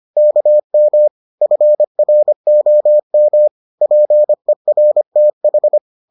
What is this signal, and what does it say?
Morse code, heard as a keyed tone, about 25 wpm, 600 Hz. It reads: KM FROM PERTH